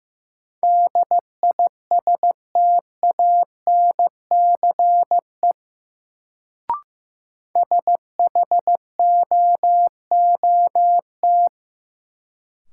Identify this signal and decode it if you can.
Morse code, heard as a keyed tone, about 15 wpm, 700 Hz. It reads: DISTANCE SHOOT
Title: Sentences from Top 1200 Words - New Word Drill - 15wpm